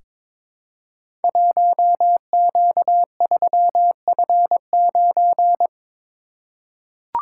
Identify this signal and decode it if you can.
Morse code, heard as a keyed tone, about 23 wpm, 700 Hz. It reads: 1Q3F9
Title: Code Groups - 5 Letter:Numbers - 22wpm